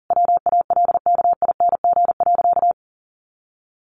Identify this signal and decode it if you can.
Morse code, heard as a keyed tone, about 40 wpm, 700 Hz. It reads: WALKING.
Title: Easy Phrases 40wpm